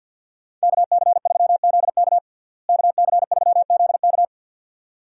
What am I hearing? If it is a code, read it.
Morse code, heard as a keyed tone, about 46 wpm, 700 Hz. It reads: KC3ZK KC3ZK
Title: USA Call Signs - 5 Characters - Repeated 2x - 50wpm